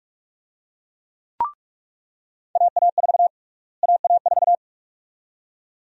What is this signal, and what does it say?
Morse code, heard as a keyed tone, about 46 wpm, 700 Hz. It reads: AA4 AA4
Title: USA Call Sign Prefixes - Repeated 2x - 45wpm